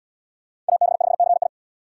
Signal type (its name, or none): Morse code